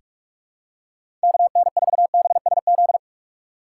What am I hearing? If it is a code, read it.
Morse code, heard as a keyed tone, about 46 wpm, 700 Hz. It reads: KN4BSZ
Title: USA Call Signs - 6 Characters - 2x Character Spacing - 45wpm